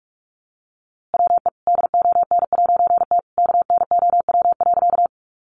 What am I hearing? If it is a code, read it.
Morse code, heard as a keyed tone, about 44 wpm, 700 Hz. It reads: WE DON'T KNOW.
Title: A Sound Of Thunder 45wpm